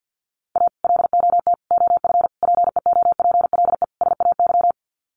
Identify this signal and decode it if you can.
Morse code, heard as a keyed tone, about 50 wpm, 700 Hz. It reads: A LOT OF PEOPLE SAY
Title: Sets of 5 Words Encore2 50wpm